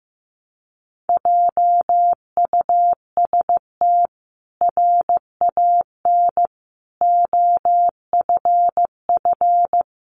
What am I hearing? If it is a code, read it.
Morse code, heard as a keyed tone, about 15 wpm, 700 Hz. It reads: JUST RAN OFF
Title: A Sound Of Thunder 15wpm